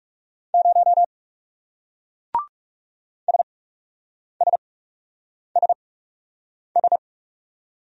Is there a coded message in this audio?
Yes: Morse code from 0.5 to 7.0 s, heard as a keyed tone, about 40 wpm, spaced as for 8 wpm, 700 Hz. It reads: 0 SSSS